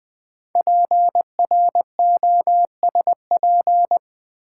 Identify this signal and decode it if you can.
Morse code, heard as a keyed tone, about 20 wpm, 700 Hz. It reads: PROSP
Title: Build A Word - 5 Letters At A Time - 10-Letter Words - 20wpm